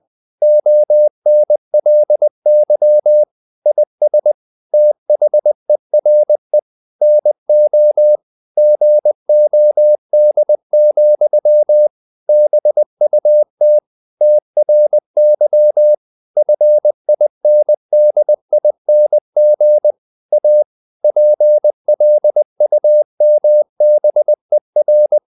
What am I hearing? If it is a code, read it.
Morse code, heard as a keyed tone, about 20 wpm, 600 Hz. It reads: ONLY IS THERE NO GOD, BUT TRY FINDING A PLUMBER
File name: CW / Practice / EnglishQuotations0320